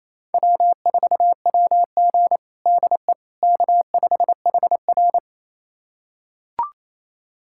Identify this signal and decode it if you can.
Morse code, heard as a keyed tone, about 28 wpm, 700 Hz. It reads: W4WG DE K5HR
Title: USA Callsign DE Callsign - 4 Characters - 28wpm